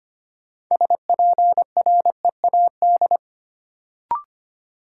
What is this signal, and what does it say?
Morse code, heard as a keyed tone, about 25 wpm, 700 Hz. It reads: SPREAD
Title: Sentences from Top 1200 Words - New Word Drill Alt - 25wpm